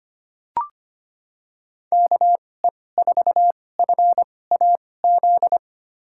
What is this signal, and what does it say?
Morse code, heard as a keyed tone, about 25 wpm, spaced as for 18 wpm, 700 Hz. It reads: KE4FAZ
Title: USA Call Signs - 6 Characters - 2x Character Spacing - 25wpm